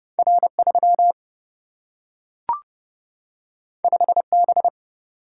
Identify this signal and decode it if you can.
Morse code, heard as a keyed tone, about 30 wpm, 700 Hz. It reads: R3 5B
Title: Single-Letter Single-Number 30wpm